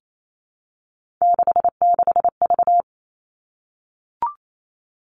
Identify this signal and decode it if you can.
Morse code, heard as a keyed tone, about 28 wpm, 700 Hz. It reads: 66V
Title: Three Challenging Letter-Number Combinations 28wpm